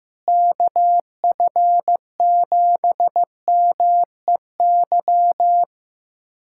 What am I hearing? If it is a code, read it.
Morse code, heard as a keyed tone, about 15 wpm, 700 Hz. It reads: KF7MEY
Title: USA Callsign DE Callsign - 6 Characters - 15wpm